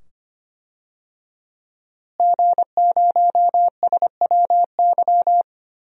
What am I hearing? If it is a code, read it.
Morse code, heard as a keyed tone, about 25 wpm, 700 Hz. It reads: G0SWY
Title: DXCC-100 Call Signs - 5 Characters - Repeated 2x - 25wpm